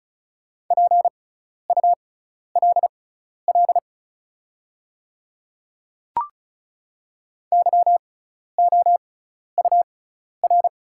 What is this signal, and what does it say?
Morse code, heard as a keyed tone, about 36 wpm, spaced as for 12 wpm, 700 Hz. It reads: PULL YOUR
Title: 6x Character Spacing - Top 500 Words - 35wpm